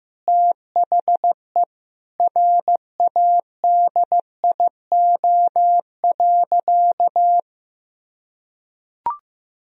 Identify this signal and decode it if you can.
Morse code, heard as a keyed tone, about 15 wpm, 700 Hz. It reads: THE RADIO.
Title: Sentences from Top 1000 Words - Review Part 3 - 15wpm